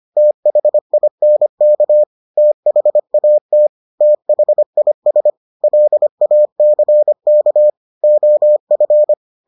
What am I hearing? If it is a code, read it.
Morse code, heard as a keyed tone, about 25 wpm, 600 Hz. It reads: THINK THAT THIS LACK OF